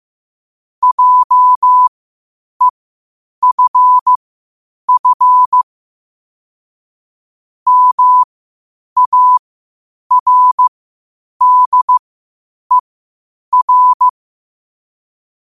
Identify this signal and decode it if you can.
Morse code, heard as a keyed tone, about 15 wpm, spaced as for 8 wpm, 1000 Hz. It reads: JEFF MARDER
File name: qotdfiles / 8wpm